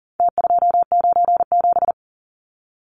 Morse code, heard as a keyed tone, about 40 wpm, 700 Hz. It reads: T297
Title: Numbers - Contest Style - 4-Digit 40wpm